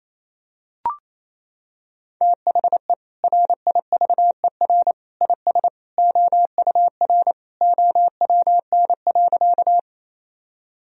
Morse code, heard as a keyed tone, about 28 wpm, 700 Hz. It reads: THE RIVER IS OUR OWN.
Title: Sentences from Top 1000 Words - Review Part 1 - 28wpm